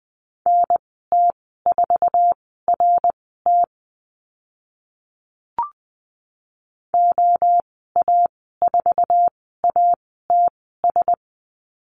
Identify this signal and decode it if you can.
Morse code, heard as a keyed tone, about 20 wpm, spaced as for 14 wpm, 700 Hz. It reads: NT4RT OA4ATS
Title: 062 Lesson 11g - Letter h - Call Signs 20wpm